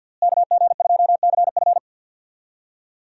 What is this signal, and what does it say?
Morse code, heard as a keyed tone, about 46 wpm, 700 Hz. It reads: KG2CF
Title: USA Call Signs - 5 Characters - Repeated 3x - 50wpm